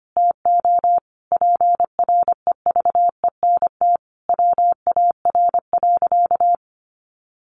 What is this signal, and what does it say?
Morse code, heard as a keyed tone, about 25 wpm, 700 Hz. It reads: TO PREVENT WAR.
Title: Sentences from Top 1100 Words 25wpm